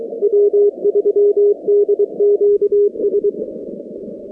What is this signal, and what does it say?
Morse code, heard as a keyed tone, about 23 wpm, 410 Hz. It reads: W3DQS
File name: contests / 2019-cw-wpx-kt5j-k5pi- -2330 0-46